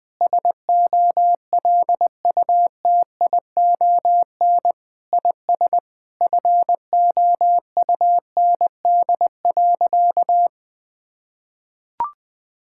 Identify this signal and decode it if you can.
Morse code, heard as a keyed tone, about 20 wpm, 700 Hz. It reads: SOLUTION IS FOUND.
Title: Sentences with 9-Word Phrases - 20wpm